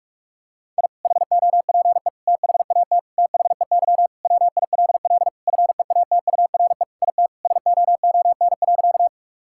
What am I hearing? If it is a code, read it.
Morse code, heard as a keyed tone, about 41 wpm, 700 Hz. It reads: I HOPE THAT THEY WILL FEATURE IT SOON.
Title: Sentences from Top 1000 Words - Review Part 5 - 45wpm